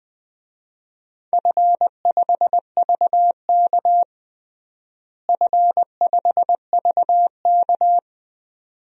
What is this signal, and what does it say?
Morse code, heard as a keyed tone, about 20 wpm, 700 Hz. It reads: F5VK F5VK
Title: DXCC-100 Call Signs - 4 Characters - Repeated 2x - 20wpm